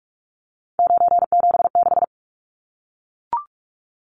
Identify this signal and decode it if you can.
Morse code, heard as a keyed tone, about 44 wpm, 700 Hz. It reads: 976
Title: Numbers - 3-Digit 45wpm